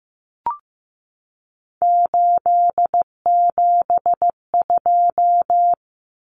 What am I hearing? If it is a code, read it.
Morse code, heard as a keyed tone, about 15 wpm, 700 Hz. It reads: 872
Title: Common Cut Numbers - 3-Digit 15wpm